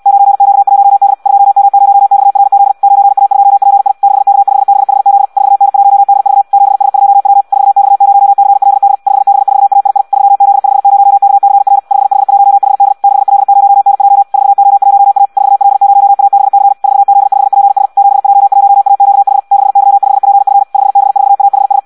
Morse code, heard as a keyed tone, about 74 wpm, 800 Hz. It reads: 9Q1D 2M0XRZ 9A2PI 6K5BHZ 4N1DV 7S2W 4X0CLD 5B4AII 3Z50KCR 5H1HD 6L0NJ 4O1A 4L0ABC 4Z5PS 7J1AQH 6Y5PF 5B4AHS